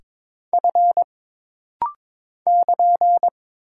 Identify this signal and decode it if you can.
Morse code, heard as a keyed tone, about 22 wpm, 700 Hz. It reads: F <KN>
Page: This rapid-fire timing is geared to push your ICR (Instant Character Recognition) skills.